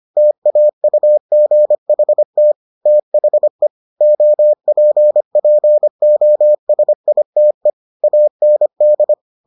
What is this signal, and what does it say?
Morse code, heard as a keyed tone, about 25 wpm, 600 Hz. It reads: TAUGHT THE OPPOSITE AND